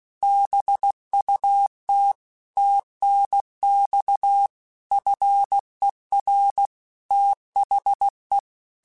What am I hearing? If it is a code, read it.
Morse code, heard as a keyed tone, about 16 wpm, 800 Hz. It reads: BUT TNX FER THE
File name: QSOs/QSOs-mp3s/kc7haa@15